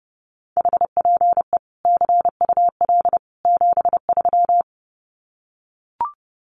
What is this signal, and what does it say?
Morse code, heard as a keyed tone, about 30 wpm, 700 Hz. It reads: HPE CUL 73
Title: Common QSO Phrases 30wpm